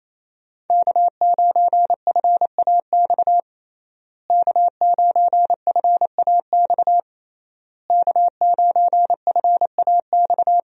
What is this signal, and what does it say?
Morse code, heard as a keyed tone, about 28 wpm, 700 Hz. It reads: K9FAX K9FAX K9FAX
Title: USA Call Signs - 5 Characters - Repeated 3x - 28wpm